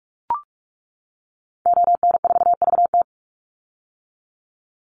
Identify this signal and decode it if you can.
Morse code, heard as a keyed tone, about 44 wpm, 700 Hz. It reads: ON4VT
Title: Call Signs - 45wpm